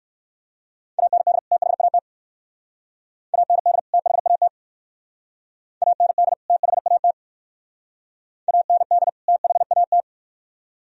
Morse code, heard as a keyed tone, about 56 wpm, 700 Hz. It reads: AND THAT AND THAT AND THAT AND THAT
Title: Speed-Racing - Sets of 2 Words 45wpm